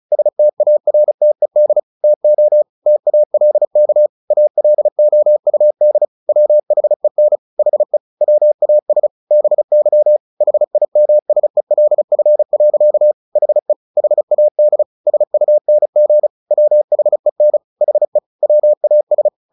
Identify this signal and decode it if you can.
Morse code, heard as a keyed tone, about 34 wpm, 600 Hz. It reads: STARTED TO TALK ALOUD WHEN HE WAS BY HIMSELF. HE HAD SUNG WHEN HE WAS
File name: the old man and the sea / the old man and the sea 1 35wpm